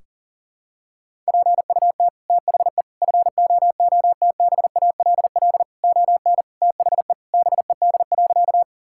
Morse code, heard as a keyed tone, about 40 wpm, 700 Hz. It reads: PUT THE FOOTBALL ON THE BED.
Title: Sentences from Top 1600 Words 40wpm